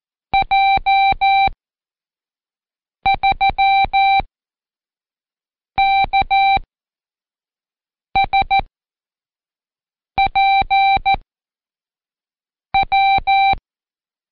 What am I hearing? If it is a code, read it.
Morse code, heard as a keyed tone, about 14 wpm, 780 Hz. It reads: J 3 K S P W
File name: Code Quick / AudioDownld1 / Seg74-Day8-Pract 1234